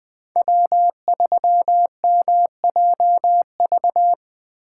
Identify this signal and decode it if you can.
Morse code, heard as a keyed tone, about 25 wpm, 700 Hz. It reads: W3MJV
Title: Speed Racing - USA 1x3 Call Signs 20wpm